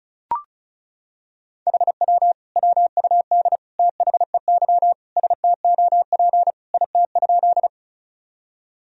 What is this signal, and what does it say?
Morse code, heard as a keyed tone, about 36 wpm, 700 Hz. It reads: HW WUD THEY STOP IT?
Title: Sentences from Top 300 Words and Abbreviations 35wpm